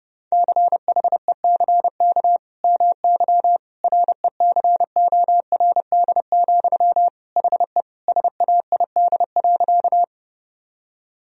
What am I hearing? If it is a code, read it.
Morse code, heard as a keyed tone, about 30 wpm, 700 Hz. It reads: CHECK MY RECORD, HE SAID.
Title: Sentences from Top 1000 Words - Review Part 3 - 30wpm